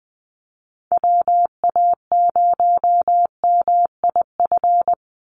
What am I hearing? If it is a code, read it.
Morse code, heard as a keyed tone, about 20 wpm, 700 Hz. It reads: WA0MIF
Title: USA Call Signs 20wpm